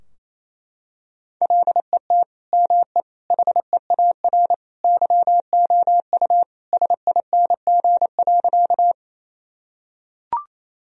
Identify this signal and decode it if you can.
Morse code, heard as a keyed tone, about 28 wpm, 700 Hz. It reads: LET ME HEAR YOU SING.
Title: Sentences from Top 1000 Words - Review Part 2 - 28wpm